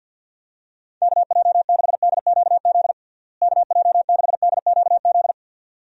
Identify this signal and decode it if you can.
Morse code, heard as a keyed tone, about 51 wpm, 700 Hz. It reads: KJ6DQZ KJ6DQZ
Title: USA Call Signs - 6 Characters - Repeated 2x - 50wpm